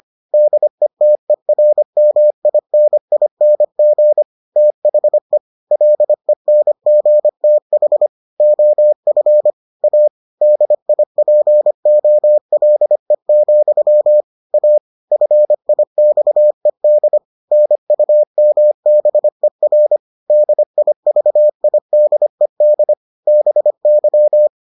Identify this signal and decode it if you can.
Morse code, heard as a keyed tone, about 25 wpm, 600 Hz. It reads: DETERMINING THE LENGTH OF A DIPOLE, A FIXED NUMBER DIVIDED BY